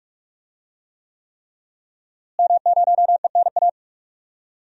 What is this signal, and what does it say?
Morse code, heard as a keyed tone, about 41 wpm, 700 Hz. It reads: M0ENA
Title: DXCC-100 Call Signs - 5 Characters - Repeated 3x - 45wpm